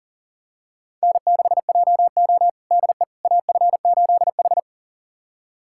Morse code, heard as a keyed tone, about 40 wpm, 700 Hz. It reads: N6JO DE AF8H
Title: USA Callsign DE Callsign - 4 Characters - 40wpm